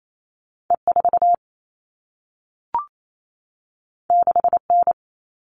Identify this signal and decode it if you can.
Morse code, heard as a keyed tone, about 28 wpm, 700 Hz. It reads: E4 6N
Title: All Cut Numbers - 2-Digit 28wpm